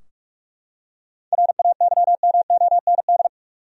Morse code, heard as a keyed tone, about 46 wpm, 700 Hz. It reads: RAYMOND
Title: USA Common Names - 45wpm